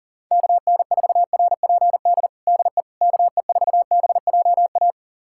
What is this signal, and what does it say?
Morse code, heard as a keyed tone, about 37 wpm, 700 Hz. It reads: KN4RPD DE KE4BJA
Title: USA Callsign DE Callsign - 6 Characters - 40wpm